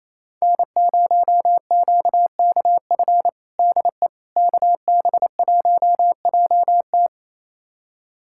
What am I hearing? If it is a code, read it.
Morse code, heard as a keyed tone, about 28 wpm, 700 Hz. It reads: N0QKF DE KB1JT
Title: USA Callsign DE Callsign - 5 Characters - 28wpm